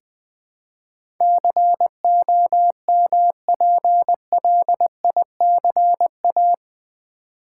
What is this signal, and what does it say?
Morse code, heard as a keyed tone, about 20 wpm, 700 Hz. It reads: COMPLICA
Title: Build A Word - 4 Letters At A Time - 11-Letter Words - 20wpm